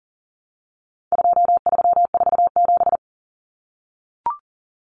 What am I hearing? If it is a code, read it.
Morse code, heard as a keyed tone, about 40 wpm, 700 Hz. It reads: 2347
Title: Numbers - 4-Digit 40wpm